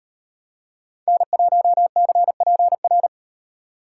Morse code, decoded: N1CPR

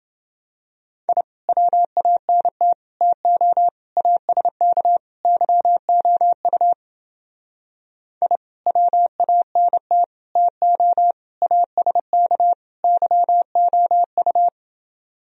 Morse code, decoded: I WANT TO ASK YOU I WANT TO ASK YOU

